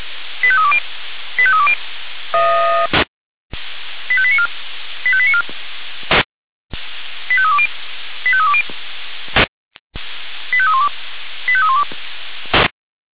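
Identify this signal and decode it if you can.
Five-tone selective call (ZVEI), heard as quick runs of five tones, tones of about 70 ms. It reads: 85320 85320 85894 85894 85320 85320 85312 85312
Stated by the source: ZVEI five-tone selective call 85312, received on SDR.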